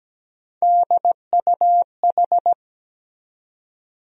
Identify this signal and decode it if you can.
Morse code, heard as a keyed tone, about 17 wpm, 700 Hz. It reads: DUH
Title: USA Call Sign Postfixes - 3 Letters 17wpm